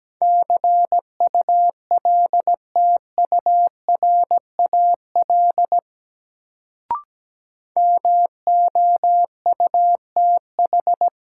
Morse code, heard as a keyed tone, about 17 wpm, 700 Hz. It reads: CULTURAL MOUTH
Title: Sentences from Top 1500 Words - New Word Drill Alt - 17wpm